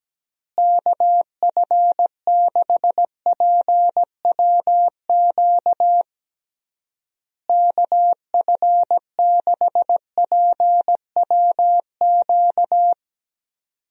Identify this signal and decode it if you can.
Morse code, heard as a keyed tone, about 17 wpm, 700 Hz. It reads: KF6PWQ KF6PWQ
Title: USA Call Signs - 6 Characters - Repeated 3x - 17wpm